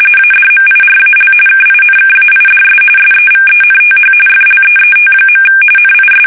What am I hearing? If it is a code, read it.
RTTY, about 45.45 baud. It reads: ADMIRAL KINKAIDS SEVENTH FLEET. NIS
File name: fsk850 / battleshipnj